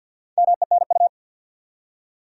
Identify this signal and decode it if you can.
Morse code, heard as a keyed tone, about 52 wpm, 700 Hz. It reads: MENU
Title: Four Letter Words - 50wpm